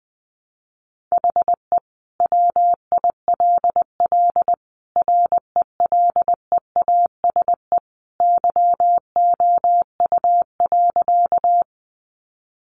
Morse code, heard as a keyed tone, about 20 wpm, 700 Hz. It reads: HE WILL RELEASE YOU.
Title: Sentences from Top 1100 Words 20wpm